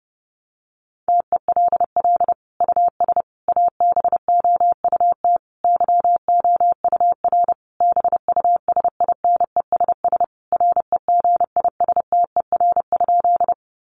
Morse code, heard as a keyed tone, about 30 wpm, 700 Hz. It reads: TELL US ABOUT YOUR BUSINESS REGISTER?
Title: Sentences from Top 1200 Words 30wpm